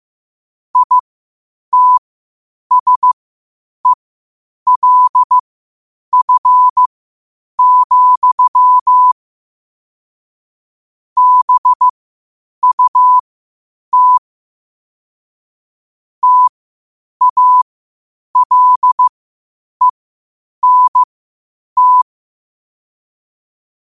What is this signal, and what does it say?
Morse code, heard as a keyed tone, about 15 wpm, spaced as for 8 wpm, 1000 Hz. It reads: ITSELF, BUT TALENT
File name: qotdfiles / 8wpm